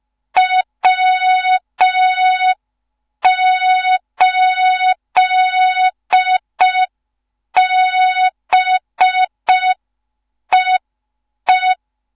Morse code, heard as a keyed tone, about 5 wpm, 3700 Hz. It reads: W8BEE
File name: Morze-HA2MN / hang / 15 GYAKORLAT